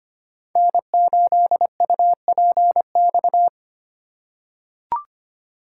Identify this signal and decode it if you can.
Morse code, heard as a keyed tone, about 25 wpm, 700 Hz. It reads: N8UPX
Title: USA Call Signs - 5 Characters - 2x Character Spacing - 25wpm